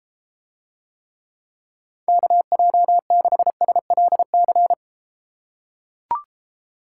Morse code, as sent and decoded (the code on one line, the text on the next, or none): -.- .--- -.... ... .-.. -.-.
KJ6SLC